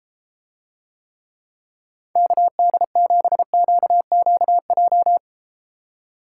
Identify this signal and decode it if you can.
Morse code, heard as a keyed tone, about 33 wpm, 700 Hz. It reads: KD7QQJ